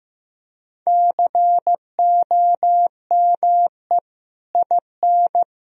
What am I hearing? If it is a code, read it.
Morse code, heard as a keyed tone, about 16 wpm, 700 Hz. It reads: COME IN
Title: Sentences from Top 1000 Words - Review Part 4 - 15wpm